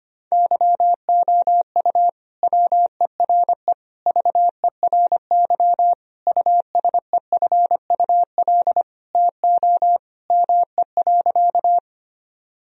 Morse code, heard as a keyed tone, about 25 wpm, 700 Hz. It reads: YOU WERE VERY USEFUL TO ME.
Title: Sentences from Top 1000 Words - Review Part 4 - 25wpm